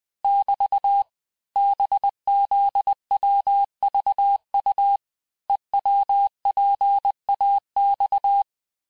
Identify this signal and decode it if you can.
Morse code, heard as a keyed tone, about 20 wpm, 800 Hz. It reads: <BT> BZWVU EWPAX
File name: cours-858-texte-chiffre-vitesse-19mpm